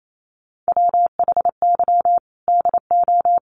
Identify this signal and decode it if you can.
Morse code, heard as a keyed tone, about 28 wpm, 700 Hz. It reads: WHY DO
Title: Sets of 5 Words Encore 28wpm